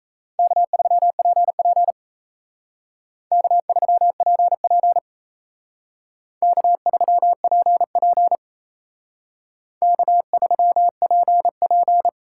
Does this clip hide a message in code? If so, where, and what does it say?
Yes: Morse code from 0.4 to 12.1 s, heard as a keyed tone, about 35 wpm, 700 Hz. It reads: K3PP K3PP K3PP K3PP